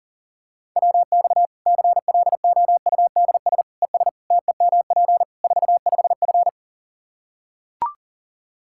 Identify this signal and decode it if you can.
Morse code, heard as a keyed tone, about 40 wpm, 700 Hz. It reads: WX CLOUDS ES TEMP 45F
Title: Practice QSO - Exchange 2 - Info - 40wpm